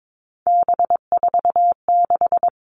Morse code, heard as a keyed tone, about 22 wpm, 700 Hz. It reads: B46
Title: All Cut Numbers - 3-Digit 22wpm